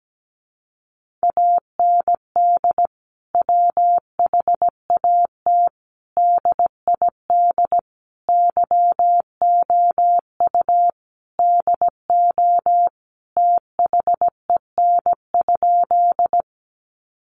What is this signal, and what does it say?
Morse code, heard as a keyed tone, about 17 wpm, 700 Hz. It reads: AND WHAT DID YOU DO THEN?
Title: Sentences from Top 100 Words 17wpm